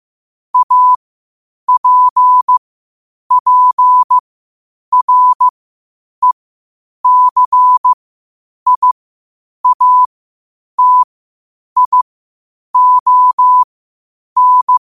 Morse code, heard as a keyed tone, about 15 wpm, spaced as for 8 wpm, 1000 Hz. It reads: APPRECIATION
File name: 8wpm